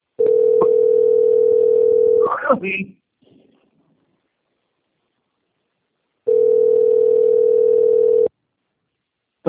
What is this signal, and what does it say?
Ringback tone (North America), 440 + 480 Hz, with speech.
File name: out- - - - - .6358